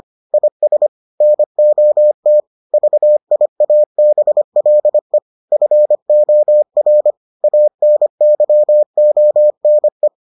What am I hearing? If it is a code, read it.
Morse code, heard as a keyed tone, about 25 wpm, 600 Hz. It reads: IS NOT VIABLE FOR ANYONE